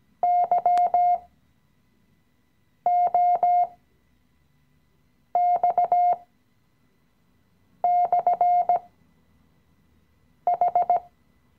Morse code, heard as a keyed tone, about 17 wpm, spaced as for 5 wpm, 710 Hz. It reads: YOX/H
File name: CW-Online-Kurs / Anfänger / Lektion 017